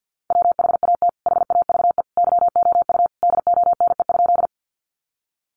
Morse code, heard as a keyed tone, about 50 wpm, 700 Hz. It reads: WHAT HAVE YOU DONE?
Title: Sentences from Top 500 Words 50wpm